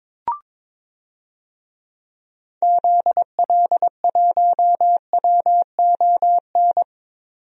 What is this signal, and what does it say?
Morse code, heard as a keyed tone, about 22 wpm, 700 Hz. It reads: ZL1WON
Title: DXCC-100 Call Signs - 6 Characters - 22wpm